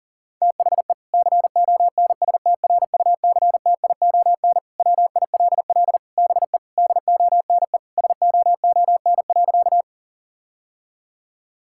Morse code, heard as a keyed tone, about 40 wpm, 700 Hz. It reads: THE CONSTRUCTION WILL BE DONE SOON.
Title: Sentences from Top 1600 Words 40wpm